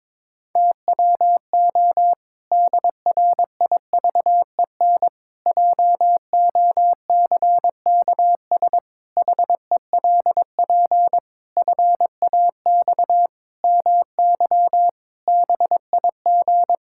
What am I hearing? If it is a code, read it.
Morse code, heard as a keyed tone, about 22 wpm, 700 Hz. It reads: TWO DRIVEN JOCKS HELP FAX MY BIG
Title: Pangram Challenge v2 - 22wpm